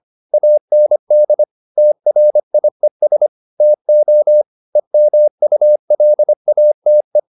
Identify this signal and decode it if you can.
Morse code, heard as a keyed tone, about 25 wpm, 600 Hz. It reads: AND TRIES TO EMULATE